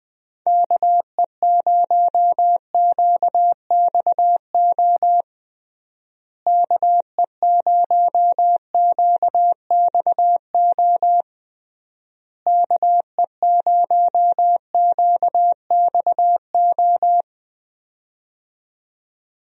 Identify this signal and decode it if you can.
Morse code, heard as a keyed tone, about 21 wpm, 700 Hz. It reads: KE0QXO KE0QXO KE0QXO
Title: USA Call Signs - 6 Characters - Repeated 3x - 20wpm